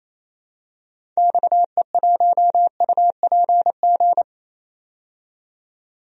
Morse code, heard as a keyed tone, about 28 wpm, 700 Hz. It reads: XE1UPG
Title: DXCC-100 Call Signs - 6 Characters - 28wpm